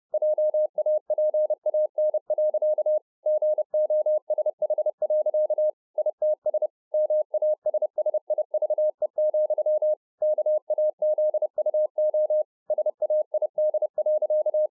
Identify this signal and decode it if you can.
Morse code, heard as a keyed tone, about 30 wpm, 610 Hz. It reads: JAPAN. GOSH. ITS MASSIVE, KAZUO SAID.